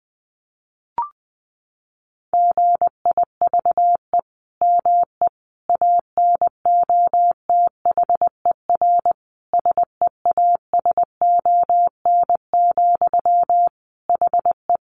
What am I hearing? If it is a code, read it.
Morse code, heard as a keyed tone, about 20 wpm, 700 Hz. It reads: GIVE ME ANOTHER SEASON, HE
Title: Sentences from Top 1400 Words 20wpm